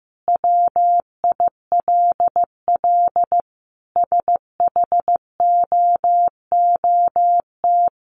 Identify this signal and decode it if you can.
Morse code, heard as a keyed tone, about 15 wpm, 700 Hz. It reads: WILL SHOOT
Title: Sentences from Top 1200 Words 15wpm